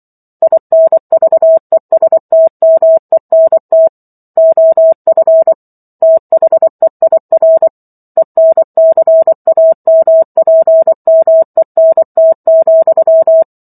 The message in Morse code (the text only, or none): INVESTMENT OF THEIR ENCAMPMENT,